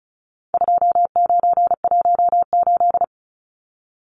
Morse code, heard as a keyed tone, about 34 wpm, 700 Hz. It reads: 2918
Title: Numbers - 4-Digit 35wpm